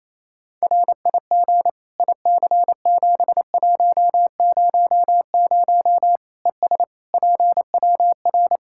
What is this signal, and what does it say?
Morse code, heard as a keyed tone, about 27 wpm, 700 Hz. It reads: RIG IC7100 ES PWR
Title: K1USN SST QSO Practice 28wpm